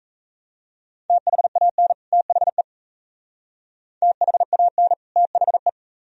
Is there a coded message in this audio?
Yes: Morse code from 1.1 to 5.7 s, heard as a keyed tone, about 39 wpm, 690 Hz. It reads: THAN THE THAN THE